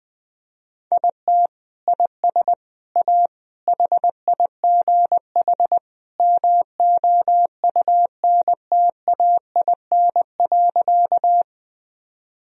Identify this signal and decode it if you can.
Morse code, heard as a keyed tone, about 20 wpm, 700 Hz. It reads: IT IS A HIGH MOUNTAIN.
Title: Sentences from Top 1000 Words - Review Part 1 - 20wpm